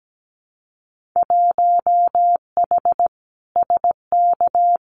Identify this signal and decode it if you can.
Morse code, heard as a keyed tone, about 17 wpm, 700 Hz. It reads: 1H SK
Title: Winter Field Day Exchanges 17wpm